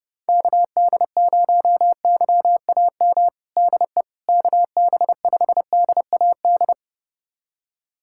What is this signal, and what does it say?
Morse code, heard as a keyed tone, about 30 wpm, 700 Hz. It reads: KD0YAM DE KB5DAD
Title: USA Callsign DE Callsign - 6 Characters - 30wpm